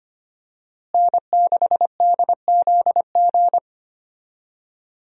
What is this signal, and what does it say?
Morse code, heard as a keyed tone, about 25 wpm, 700 Hz. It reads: N6DZG
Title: USA Call Signs - 5 Characters - Repeated 3x - 25wpm